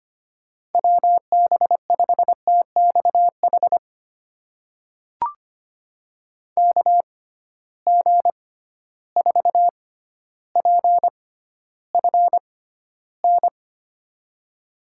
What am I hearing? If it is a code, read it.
Morse code, heard as a keyed tone, about 25 wpm, spaced as for 13 wpm, 700 Hz. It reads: WB5TXH KG4PFN
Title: USA Call Signs - 6 Characters - 6x Character Spacing - 25wpm